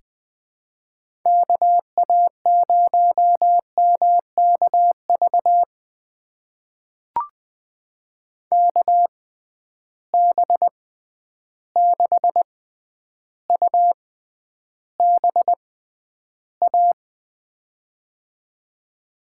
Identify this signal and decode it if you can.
Morse code, heard as a keyed tone, about 21 wpm, spaced as for 10 wpm, 700 Hz. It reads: KA0MKV KB6UBA
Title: USA Call Signs - 6 Characters - 6x Character Spacing - 20wpm